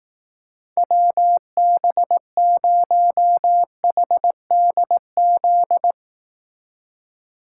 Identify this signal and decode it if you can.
Morse code, heard as a keyed tone, about 18 wpm, 700 Hz. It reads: WB0HDZ
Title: Speed Racing - USA 2x3 Call Signs 15wpm